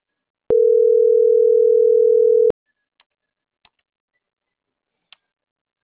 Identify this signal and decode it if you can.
Ringback tone (North America), 440 + 480 Hz.